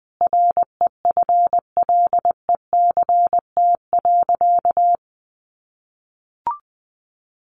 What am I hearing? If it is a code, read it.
Morse code, heard as a keyed tone, about 20 wpm, 700 Hz. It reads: REFLECT.
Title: Sentences from Top 1500 Words 20wpm